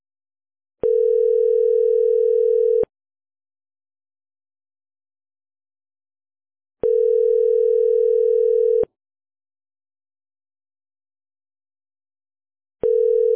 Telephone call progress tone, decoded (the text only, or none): ringback tone